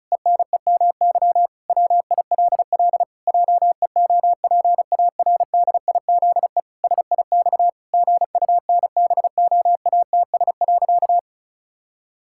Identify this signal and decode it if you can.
Morse code, heard as a keyed tone, about 36 wpm, 700 Hz. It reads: ENEMY WILL JEOPARDIZE SIX GUNBOATS.
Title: Pangram Challenge v2 - 35wpm